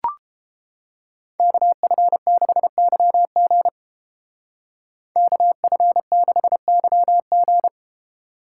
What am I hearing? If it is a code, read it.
Morse code, heard as a keyed tone, about 30 wpm, 700 Hz. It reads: KF6YG KF6YG